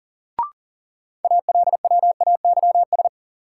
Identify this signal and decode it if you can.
Morse code, heard as a keyed tone, about 40 wpm, 700 Hz. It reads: ALWAYS
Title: Sentences from Top 300 Words - New Word Drill - 40wpm